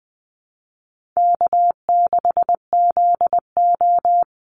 Morse code, heard as a keyed tone, about 20 wpm, 700 Hz. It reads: K6ZO
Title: Easy Call Signs Repeated 3x - 20wpm